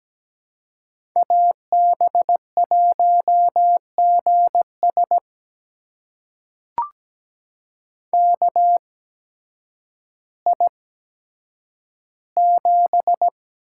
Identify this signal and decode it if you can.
Morse code, heard as a keyed tone, about 17 wpm, spaced as for 9 wpm, 700 Hz. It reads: AB1GS KI7
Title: USA Call Signs - 8x Character Spacing - 17wpm